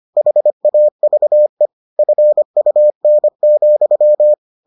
Morse code, heard as a keyed tone, about 25 wpm, 600 Hz. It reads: HAVE FUN,